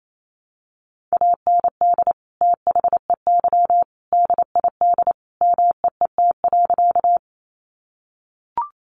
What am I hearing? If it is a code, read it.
Morse code, heard as a keyed tone, about 28 wpm, 700 Hz. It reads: AND THEY DID MEET.
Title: Sentences from Top 1100 Words 28wpm